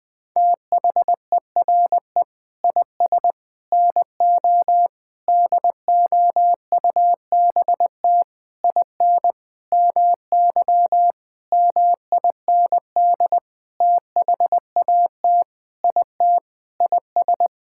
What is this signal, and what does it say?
Morse code, heard as a keyed tone, about 20 wpm, 700 Hz. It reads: THERE IS NO DOUBT IN MY MIND THAT IT IS
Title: Sentences with 6-Word Phrases - 20wpm